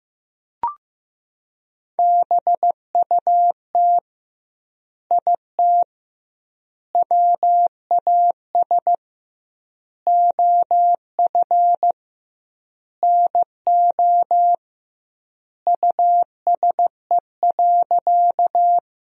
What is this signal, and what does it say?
Morse code, heard as a keyed tone, about 15 wpm, 700 Hz. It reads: BUT IT WAS OF NO USE.
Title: 2x Word-Spacing - Sentences from Top 100 Words - 15wpm